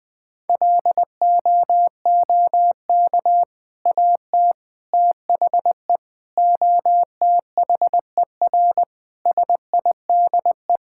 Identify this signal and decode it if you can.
Morse code, heard as a keyed tone, about 19 wpm, 700 Hz. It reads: LOOK AT THE OTHER SIDE
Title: Sentences with 3-Word Phrases - Spoken Beginning - 20wpm